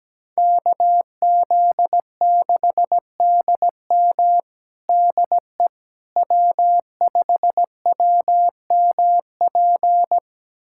Morse code, heard as a keyed tone, about 17 wpm, 700 Hz. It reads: KZ6DM DE W5WMP
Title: USA Callsign DE Callsign - 5 Characters - 17wpm